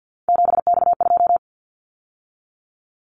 Morse code, heard as a keyed tone, about 50 wpm, 690 Hz. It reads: 7X2
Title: Callsign Prefixes for Top 100 DXCC - 50wpm